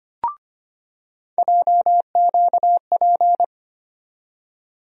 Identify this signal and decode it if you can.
Morse code, heard as a keyed tone, about 25 wpm, 700 Hz. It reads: JQP
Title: USA Call Sign Postfixes - 3 Letters 25wpm